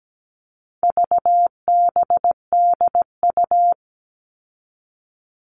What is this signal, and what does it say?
Morse code, heard as a keyed tone, about 17 wpm, 700 Hz. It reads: VBDU
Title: All Cut Numbers - 4-Digit 17wpm